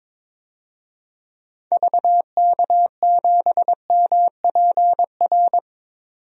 Morse code, heard as a keyed tone, about 22 wpm, 700 Hz. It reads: VK7MPR
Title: DXCC-100 Call Signs - 6 Characters - 22wpm